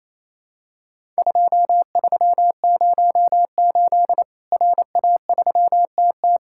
Morse code, heard as a keyed tone, about 28 wpm, 700 Hz. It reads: 2308 RA3TT
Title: Worked All Europe DX Contest - QTC Exchange - 28wpm